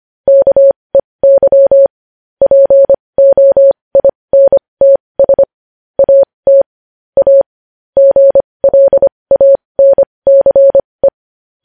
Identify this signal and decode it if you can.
Morse code, heard as a keyed tone, about 25 wpm, spaced as for 20 wpm, 550 Hz. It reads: KEY POINTS AT A GLANCE
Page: News Headlines in Morse Code at 20 WPM